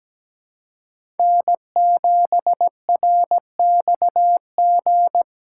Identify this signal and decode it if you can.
Morse code, heard as a keyed tone, about 17 wpm, 700 Hz. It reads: N7RXG
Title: USA Call Signs - 5 Characters - 6x Character Spacing - 17wpm